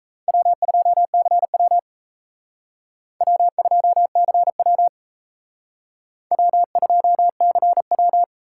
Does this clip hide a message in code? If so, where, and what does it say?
Yes: Morse code from 0.3 to 8.2 s, heard as a keyed tone, about 36 wpm, 700 Hz. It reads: W2CW W2CW W2CW